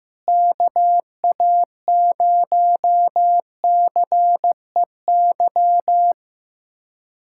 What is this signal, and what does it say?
Morse code, heard as a keyed tone, about 15 wpm, 700 Hz. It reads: KA0CEY
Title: USA Callsign DE Callsign 15wpm